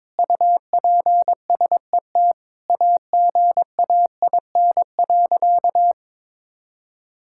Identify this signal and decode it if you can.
Morse code, heard as a keyed tone, about 21 wpm, 700 Hz. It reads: UPSET AGAIN.
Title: Sentences from Top 1000 Words - Review Part 5 - 22wpm